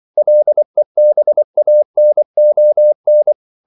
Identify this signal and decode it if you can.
Morse code, heard as a keyed tone, about 24 wpm, 600 Hz. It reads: LEBANON
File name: CW / Practice / NationalCapitals24